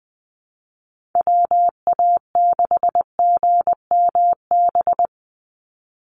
Morse code, heard as a keyed tone, about 20 wpm, 700 Hz. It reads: WA6GMB
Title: USA Call Signs 20wpm